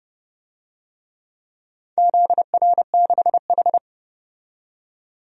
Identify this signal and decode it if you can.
Morse code, heard as a keyed tone, about 30 wpm, 700 Hz. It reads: ZR6H
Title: DXCC-100 Call Signs - 4 Characters - 30wpm